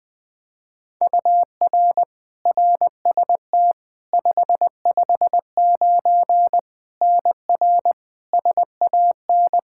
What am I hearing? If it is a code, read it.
Morse code, heard as a keyed tone, about 20 wpm, 700 Hz. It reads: UR RST 559 NR SAN
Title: K1USN SST QSO Practice 20wpm